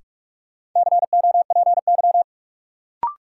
Morse code, heard as a keyed tone, about 44 wpm, 700 Hz. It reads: COPY
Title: Sentences from Top 1400 Words - New Word Drill Alt - 45wpm